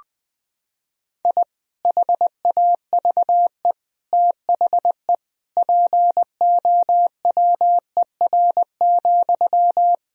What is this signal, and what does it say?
Morse code, heard as a keyed tone, about 20 wpm, 700 Hz. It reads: I HAVE THE POWER,
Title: Sentences from Top 1000 Words - Review Part 2 - 20wpm